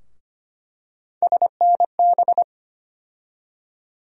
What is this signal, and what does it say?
Morse code, heard as a keyed tone, about 25 wpm, 700 Hz. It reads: SNB
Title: USA Call Sign Postfixes - 3 Letters 25wpm